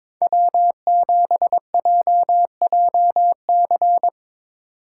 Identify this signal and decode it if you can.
Morse code, heard as a keyed tone, about 25 wpm, 700 Hz. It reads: W7JJC
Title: Speed Racing - USA 1x3 Call Signs 22wpm